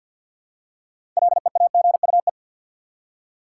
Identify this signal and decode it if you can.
Morse code, heard as a keyed tone, about 48 wpm, 700 Hz. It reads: LEAGUE
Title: Six Letter Words - 50wpm